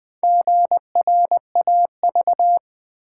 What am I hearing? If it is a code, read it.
Morse code, heard as a keyed tone, about 20 wpm, 700 Hz. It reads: GRAV
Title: Build A Word - 1 Letter At A Time - 5-Letter Words - 20wpm